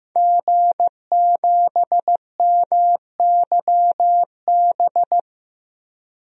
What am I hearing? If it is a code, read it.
Morse code, heard as a keyed tone, about 16 wpm, 700 Hz. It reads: G7MYB
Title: DXCC-100 Call Signs - 5 Characters - 15wpm